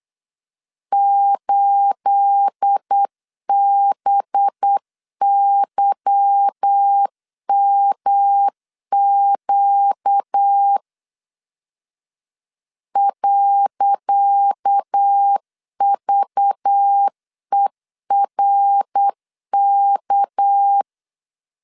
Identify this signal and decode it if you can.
Morse code, heard as a keyed tone, about 8 wpm, 800 Hz. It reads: 8BYMQ .VERK
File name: LEZIONI / 10 SIGNS / 10C DOT 8wpm 3649